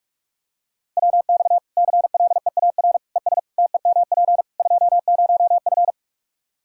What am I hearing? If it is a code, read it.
Morse code, heard as a keyed tone, about 46 wpm, 700 Hz. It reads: WX CLEAR ES TEMP 20F
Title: Practice QSO - Exchange 2 - Info - 45wpm